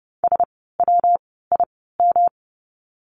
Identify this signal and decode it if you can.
Morse code, heard as a keyed tone, about 30 wpm, 700 Hz. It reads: S W I M
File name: course / 30:10wpm / 115 Lesson 19f - Letter w - Words 30wpm